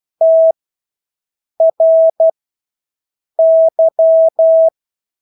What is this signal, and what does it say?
Morse code, heard as a keyed tone, about 12 wpm, spaced as for 6 wpm, 650 Hz. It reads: TRY
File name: Engelske ord 12-6 wpm